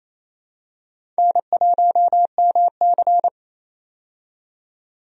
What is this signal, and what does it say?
Morse code, heard as a keyed tone, about 28 wpm, 700 Hz. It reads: N1MC
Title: USA Call Signs - 4 Characters - 28wpm